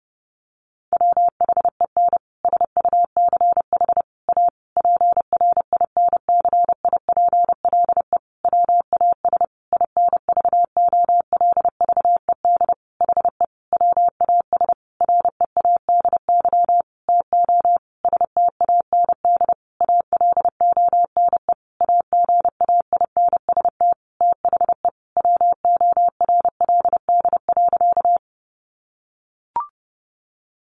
Morse code, heard as a keyed tone, about 30 wpm, 700 Hz. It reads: WHEN SUCH A PRINCIPLE WAS INVOLVED HE WAS READY TO STAND ALONE AGAINST THE WORLD.
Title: Sentences from Top 1400 Words 30wpm